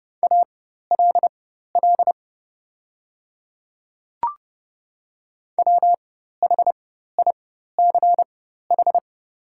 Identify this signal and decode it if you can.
Morse code, heard as a keyed tone, about 29 wpm, spaced as for 14 wpm, 700 Hz. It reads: ALL WHICH
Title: 4x Character Spacing - Top 200 Words - 30wpm